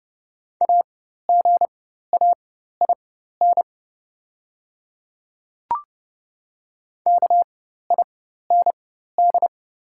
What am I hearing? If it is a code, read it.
Morse code, heard as a keyed tone, about 30 wpm, spaced as for 14 wpm, 700 Hz. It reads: AGAIN KIND
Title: 4x Character Spacing - Top 200 Words - 30wpm